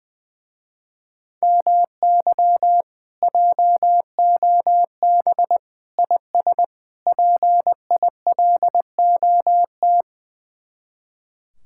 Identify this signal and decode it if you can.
Morse code, heard as a keyed tone, about 21 wpm, 700 Hz. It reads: MY JOB IS PILOT
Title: Speed-Racing Common QSO Phrases - 20wpm